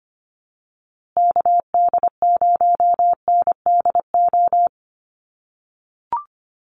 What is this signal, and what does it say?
Morse code, heard as a keyed tone, about 25 wpm, 700 Hz. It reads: KD0NDO
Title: USA Call Signs Repeated 3x 25wpm